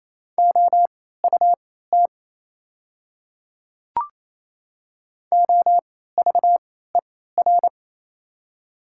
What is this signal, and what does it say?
Morse code, heard as a keyed tone, about 28 wpm, spaced as for 16 wpm, 700 Hz. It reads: OUT OVER
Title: 3x Character Spacing - Top 100 Words - 28wpm